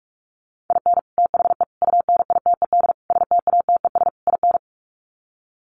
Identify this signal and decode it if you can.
Morse code, heard as a keyed tone, about 44 wpm, 700 Hz. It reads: IN THE UNITED STATES IN
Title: Sets of 5 Words 45wpm